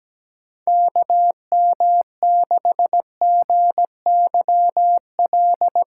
Morse code, heard as a keyed tone, about 17 wpm, 700 Hz. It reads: KM6GYL
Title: USA Call Signs - 6 Characters - 6x Character Spacing - 17wpm